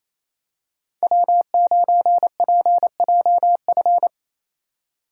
Morse code, heard as a keyed tone, about 28 wpm, 700 Hz. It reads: W9PJF
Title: USA Call Signs - 5 Characters - 2x Character Spacing - 28wpm